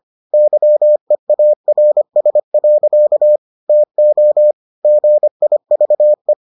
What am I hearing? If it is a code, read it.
Morse code, heard as a keyed tone, about 25 wpm, 600 Hz. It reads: YEARS. TO GIVE